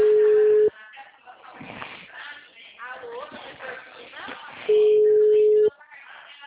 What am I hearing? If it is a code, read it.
Ringback tone (Europe, 425 Hz), with speech.